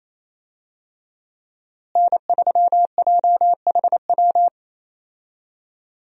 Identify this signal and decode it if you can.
Morse code, heard as a keyed tone, about 28 wpm, 700 Hz. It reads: N3JHW
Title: USA Call Signs - 5 Characters - Repeated 2x - 28wpm